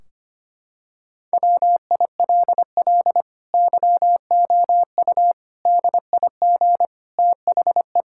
Morse code, heard as a keyed tone, about 25 wpm, 700 Hz. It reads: WILL YOU DIG THE
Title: Sentences from Top 1600 Words 25wpm